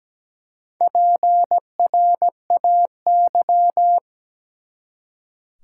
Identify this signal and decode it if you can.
Morse code, heard as a keyed tone, about 17 wpm, 700 Hz. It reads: PRAY
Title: Four Letter Words - 17wpm